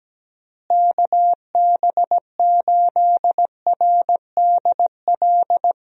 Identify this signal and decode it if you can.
Morse code, heard as a keyed tone, about 17 wpm, 700 Hz. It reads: KB8RDL
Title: USA Call Signs - 6 Characters - Repeated 2x - 17wpm